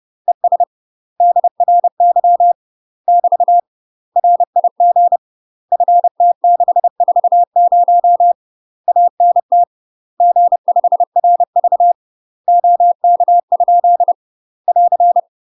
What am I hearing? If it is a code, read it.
Morse code, heard as a keyed tone, about 30 wpm, 700 Hz. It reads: ES DRY <BT> RIG FT640 ANT G5RV OK? <AR>
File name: UK MASTER QSOs TRAINING / 30WPM / UK MASTER QSOs 0015